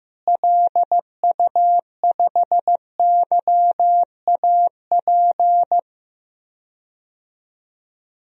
Morse code, heard as a keyed tone, about 16 wpm, 700 Hz. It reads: LU5YAP
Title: DXCC-100 Call Signs - 6 Characters - Repeated 3x - 15wpm